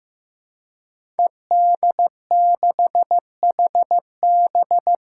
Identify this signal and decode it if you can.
Morse code, heard as a keyed tone, about 15 wpm, 700 Hz. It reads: ED6HB
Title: Five Challenging Letter-Number Combinations 15wpm